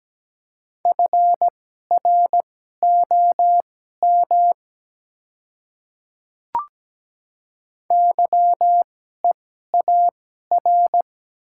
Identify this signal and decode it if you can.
Morse code, heard as a keyed tone, about 17 wpm, spaced as for 12 wpm, 700 Hz. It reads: FROM YEAR
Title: 2x Character Spacing - Top 200 Words - 17wpm